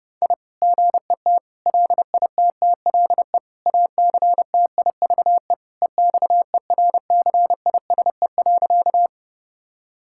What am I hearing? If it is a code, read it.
Morse code, heard as a keyed tone, about 32 wpm, 700 Hz. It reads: I GET LITTLE ACTIVE EXERCISE.
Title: Sentences from Top 1000 Words - Review Part 6 - 30wpm